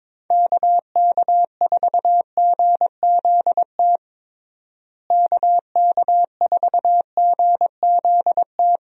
Morse code, heard as a keyed tone, about 22 wpm, 700 Hz. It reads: KK4GZT KK4GZT
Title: USA Call Signs - 6 Characters - Repeated 3x - 22wpm